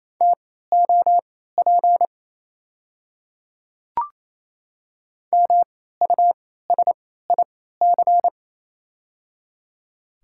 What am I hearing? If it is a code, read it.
Morse code, heard as a keyed tone, about 28 wpm, spaced as for 16 wpm, 690 Hz. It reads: TOP MUSIC
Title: 3x Character Spacing - Top 500 Words - 28wpm